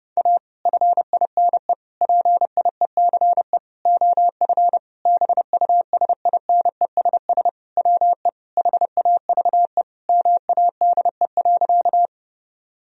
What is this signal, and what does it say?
Morse code, heard as a keyed tone, about 32 wpm, 700 Hz. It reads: A FINE PIECE OF BUSINESS WE HAVE MADE.
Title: Sentences from Top 1000 Words - Review Part 2 - 30wpm